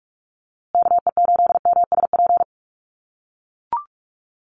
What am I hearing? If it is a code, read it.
Morse code, heard as a keyed tone, about 44 wpm, 700 Hz. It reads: KE8MSP
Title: USA Call Signs Repeated 2x 45wpm